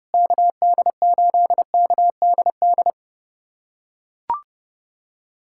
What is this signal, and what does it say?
Morse code, heard as a keyed tone, about 30 wpm, 700 Hz. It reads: KD8KDD
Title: USA Callsign DE Callsign 30wpm